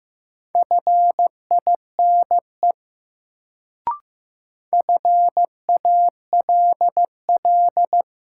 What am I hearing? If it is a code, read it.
Morse code, heard as a keyed tone, about 15 wpm, 700 Hz. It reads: FINE FALL
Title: Sentences from Top 500 Words - New Word Drill - 15wpm